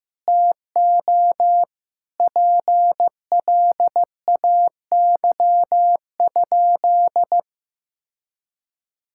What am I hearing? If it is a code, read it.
Morse code, heard as a keyed tone, about 15 wpm, 700 Hz. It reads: TO PLAY?
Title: Sentences from Top 1000 Words - Review Part 6 - 15wpm